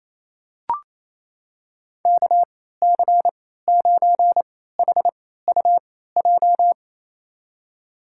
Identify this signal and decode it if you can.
Morse code, heard as a keyed tone, about 28 wpm, spaced as for 16 wpm, 700 Hz. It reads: KC9HUJ
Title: USA Call Signs - 6 Characters - 3x Character Spacing - 28wpm